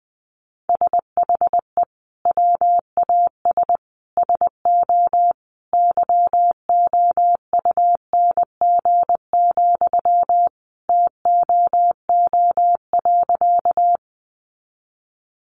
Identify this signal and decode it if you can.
Morse code, heard as a keyed tone, about 20 wpm, 700 Hz. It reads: SHE WAS SO YOUNG, TOO.
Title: Sentences from Top 400 Words 20wpm